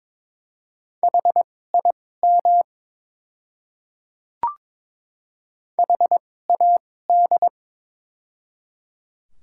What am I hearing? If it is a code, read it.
Morse code, heard as a keyed tone, about 22 wpm, spaced as for 16 wpm, 700 Hz. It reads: HIM HAD
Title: 2x Character Spacing - Top 200 Words - 22wpm